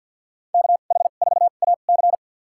Morse code, heard as a keyed tone, about 52 wpm, spaced as for 36 wpm, 700 Hz. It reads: KH4AC
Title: USA Call Signs - 2x Character Spacing - 50wpm